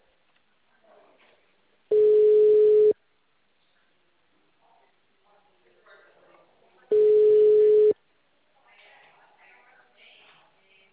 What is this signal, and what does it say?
Ringback tone (Europe, 425 Hz), with speech.
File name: out- -2039- - - .63298